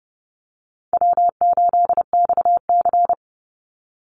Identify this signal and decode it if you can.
Morse code, heard as a keyed tone, about 30 wpm, 700 Hz. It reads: W8XC
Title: Easy Call Signs - 30wpm